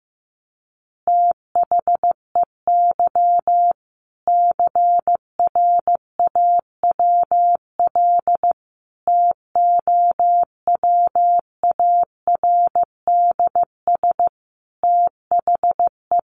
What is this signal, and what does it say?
Morse code, heard as a keyed tone, about 15 wpm, 700 Hz. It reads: THEY CRAWL TOWARDS THE
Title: Additional Easy Phrases 15wpm